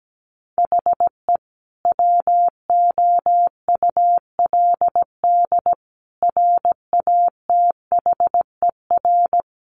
Morse code, heard as a keyed tone, about 17 wpm, 700 Hz. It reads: HE WOULD RATHER
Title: Sentences from Top 1500 Words 17wpm